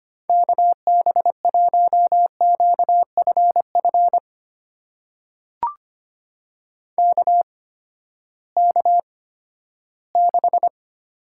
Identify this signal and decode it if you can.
Morse code, heard as a keyed tone, about 25 wpm, spaced as for 14 wpm, 700 Hz. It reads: KB1QFF KK6
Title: USA Call Signs - 6 Characters - 8x Character Spacing - 25wpm